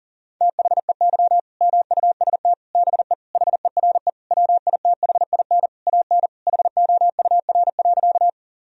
Morse code, heard as a keyed tone, about 40 wpm, 700 Hz. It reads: THEY MUST BE HERE WITHIN AN HOUR.
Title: Sentences from Top 1000 Words - Review Part 4 - 40wpm